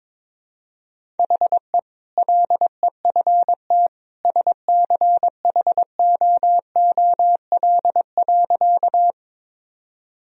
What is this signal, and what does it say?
Morse code, heard as a keyed tone, about 22 wpm, 700 Hz. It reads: HE LEFT SCHOOL.
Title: Sentences from Top 1000 Words - Review Part 1 - 22wpm